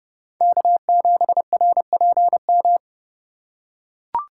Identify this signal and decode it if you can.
Morse code, heard as a keyed tone, about 30 wpm, 700 Hz. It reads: K7RPM
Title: USA Call Signs - 5 Characters - 3x Character Spacing - 30wpm